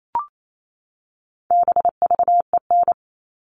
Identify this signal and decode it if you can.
Morse code, heard as a keyed tone, about 28 wpm, 700 Hz. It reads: BVEN
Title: All Cut Numbers - 4-Digit 28wpm